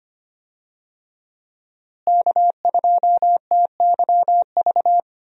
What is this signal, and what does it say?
Morse code, heard as a keyed tone, about 25 wpm, 700 Hz. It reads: K2TYV
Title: USA Call Signs - 5 Characters - Repeated 2x - 25wpm